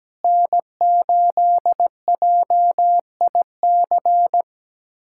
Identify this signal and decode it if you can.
Morse code, heard as a keyed tone, about 17 wpm, 700 Hz. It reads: N8JIC
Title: USA Callsign DE Callsign - 5 Characters - 17wpm